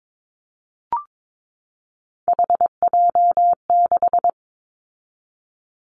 Morse code, heard as a keyed tone, about 22 wpm, 700 Hz. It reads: HJ6
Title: Callsign Prefixes for Top 100 DXCC - 22wpm